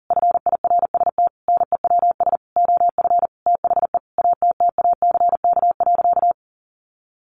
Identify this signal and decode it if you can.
Morse code, heard as a keyed tone, about 40 wpm, 700 Hz. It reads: FIRST NEWS OF THE ATTACK.
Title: Sentences from Top 1500 Words 40wpm